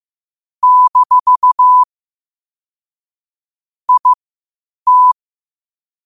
Morse code, heard as a keyed tone, about 15 wpm, spaced as for 8 wpm, 1000 Hz. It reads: - IT
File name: qotdfiles / 8wpm